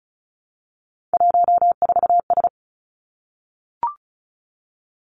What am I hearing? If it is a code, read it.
Morse code, heard as a keyed tone, about 36 wpm, 700 Hz. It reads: 14S
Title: Three Challenging Letter-Number Combinations 35wpm